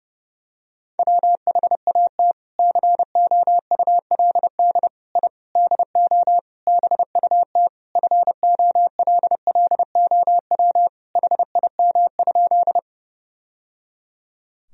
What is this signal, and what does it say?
Morse code, heard as a keyed tone, about 32 wpm, 700 Hz. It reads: WHAT COULD I DO BUT FOLLOW HIM?
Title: Sentences from Top 1000 Words - Review Part 1 - 30wpm